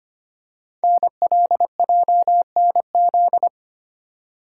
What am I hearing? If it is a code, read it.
Morse code, heard as a keyed tone, about 25 wpm, 700 Hz. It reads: NLJNZ
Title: Code Groups - 5 Letters - 25wpm